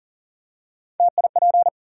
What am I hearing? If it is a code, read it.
Morse code, heard as a keyed tone, about 40 wpm, 700 Hz. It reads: TIP
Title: Three Letter Words - 40wpm